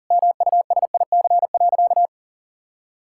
Morse code, heard as a keyed tone, about 40 wpm, 700 Hz. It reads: MUSIC.
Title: Sentences from Top 1000 Words - Review Part 6 - 40wpm